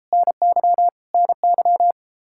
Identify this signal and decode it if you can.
Morse code, heard as a keyed tone, about 33 wpm, 700 Hz. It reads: NY NY